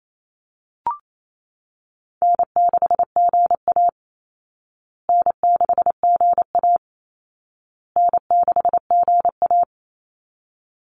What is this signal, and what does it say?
Morse code, heard as a keyed tone, about 28 wpm, 700 Hz. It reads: N6GA N6GA N6GA
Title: Easy Call Signs Repeated 3x - 28wpm